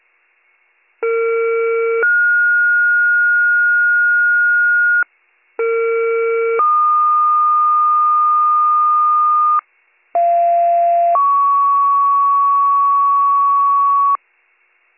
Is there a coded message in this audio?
Yes: two-tone page from 1.0 to 14.2 s, heard as a one-second tone, then a three-second tone, about 460 then 1430 Hz, about 460 then 1220 Hz, about 690 then 1120 Hz.